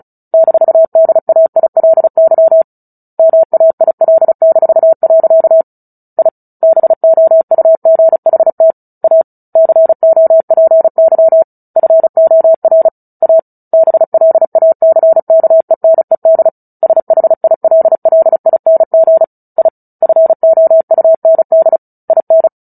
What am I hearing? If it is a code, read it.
Morse code, heard as a keyed tone, about 34 wpm, 650 Hz. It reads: -DAILY MAIL-. I BOUGHT A COPY FOR A BLACKENED SHILLING I FOUND IN
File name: Book Two / 09-Chapter 9 - 35 WPM